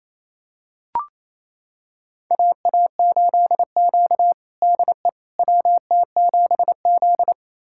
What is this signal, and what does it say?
Morse code, heard as a keyed tone, about 28 wpm, 700 Hz. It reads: AA8Q DE WT7Z
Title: USA Callsign DE Callsign - 4 Characters - 28wpm